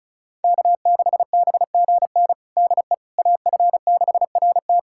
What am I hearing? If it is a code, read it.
Morse code, heard as a keyed tone, about 36 wpm, 700 Hz. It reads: K6BGN DE AF6RT
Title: USA Callsign DE Callsign - 5 Characters - 35wpm